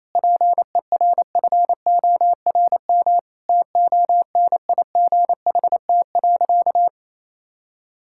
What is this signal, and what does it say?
Morse code, heard as a keyed tone, about 28 wpm, 700 Hz. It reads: PERFORM TONIGHT.
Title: Sentences from Top 1000 Words - Review Part 5 - 28wpm